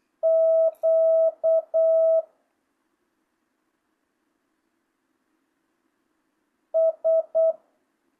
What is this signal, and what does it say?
Morse code, heard as a keyed tone, about 8 wpm, 650 Hz. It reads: Q S
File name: jh8cbh / 2025cw